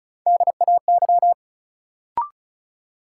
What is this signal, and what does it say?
Morse code, heard as a keyed tone, about 36 wpm, 700 Hz. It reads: DAY
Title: Sentences from Top 200 Words - New Word Drill - 35wpm